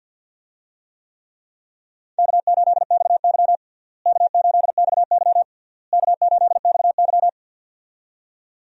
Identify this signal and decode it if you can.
Morse code, heard as a keyed tone, about 50 wpm, 700 Hz. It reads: K8XY K8XY K8XY
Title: USA Call Signs - 4 Characters - Repeated 3x - 50wpm